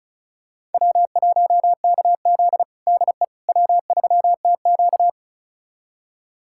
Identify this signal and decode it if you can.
Morse code, heard as a keyed tone, about 34 wpm, 700 Hz. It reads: W1KZ DE W3TQ
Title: USA Callsign DE Callsign - 4 Characters - 35wpm